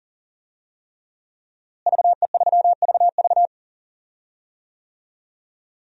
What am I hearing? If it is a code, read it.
Morse code, heard as a keyed tone, about 40 wpm, 700 Hz. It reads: VE3VV
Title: DXCC-100 Call Signs - 5 Characters - Repeated 3x - 40wpm